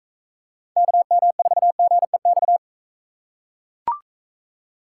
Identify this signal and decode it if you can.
Morse code, heard as a keyed tone, about 43 wpm, 700 Hz. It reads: KM4GEX